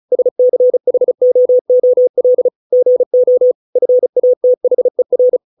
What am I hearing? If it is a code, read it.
Morse code, heard as a keyed tone, about 34 wpm, 500 Hz. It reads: SCHOOL GO FATHER